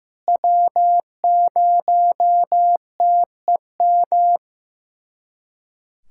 Morse code, heard as a keyed tone, about 16 wpm, 700 Hz. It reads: W0TEM
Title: SOTA QSO Practice 15wpm